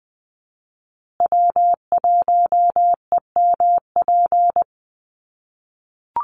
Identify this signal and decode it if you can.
Morse code, heard as a keyed tone, about 20 wpm, 700 Hz. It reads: W1EMP
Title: USA Call Signs Repeated 3x 20wpm